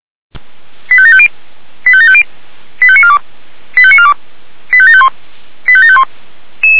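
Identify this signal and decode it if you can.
Five-tone selective call (ZVEI), heard as quick runs of five tones, tones of about 70 ms. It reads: 85750 85750 85942 85942 85751 85751
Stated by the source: ZVEI five-tone selective call received on an SDR by the Feuerwehr Orlamünde (fire brigade).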